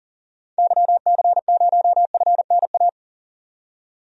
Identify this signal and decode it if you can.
Morse code, heard as a keyed tone, about 40 wpm, 700 Hz. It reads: YC0FNA
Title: DXCC-100 Call Signs - 6 Characters - 40wpm